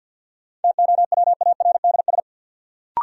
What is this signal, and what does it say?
Morse code, heard as a keyed tone, about 52 wpm, 700 Hz. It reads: TOWARDS
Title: Sentences from Top 1100 Words - New Word Drill Alt - 50wpm